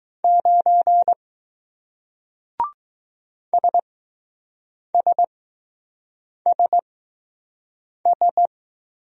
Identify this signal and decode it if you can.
Morse code, heard as a keyed tone, about 20 wpm, spaced as for 6 wpm, 700 Hz. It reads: 9 SSSS